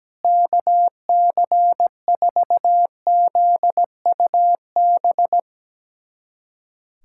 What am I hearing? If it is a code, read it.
Morse code, heard as a keyed tone, about 17 wpm, 700 Hz. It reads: KC4ZUB
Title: USA Callsign DE Callsign - 6 Characters - 17wpm